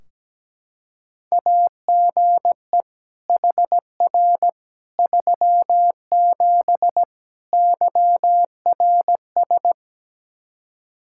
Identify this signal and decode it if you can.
Morse code, heard as a keyed tone, about 17 wpm, 700 Hz. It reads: AGE HR 37 YRS
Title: Practice QSO - Exchange 3 - More Info - 17wpm